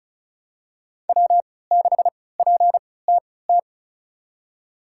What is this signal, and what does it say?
Morse code, heard as a keyed tone, about 36 wpm, spaced as for 20 wpm, 690 Hz. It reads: W6PTT
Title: USA Call Signs - 5 Characters - 3x Character Spacing - 35wpm